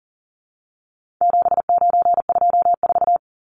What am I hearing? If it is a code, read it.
Morse code, heard as a keyed tone, about 40 wpm, 700 Hz. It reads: 7924
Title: Numbers - 4-Digit 40wpm